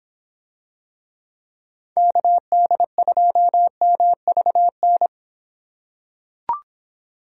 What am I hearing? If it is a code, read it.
Morse code, heard as a keyed tone, about 25 wpm, 700 Hz. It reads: KD2MVN